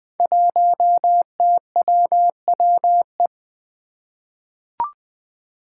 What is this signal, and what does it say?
Morse code, heard as a keyed tone, about 20 wpm, 700 Hz. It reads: 1TWWE
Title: Five Challenging Letter-Number Combinations 20wpm